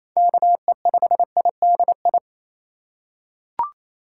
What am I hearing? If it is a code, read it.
Morse code, heard as a keyed tone, about 28 wpm, 700 Hz. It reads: KE5IDI
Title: USA Call Signs - 2x Character Spacing - 28wpm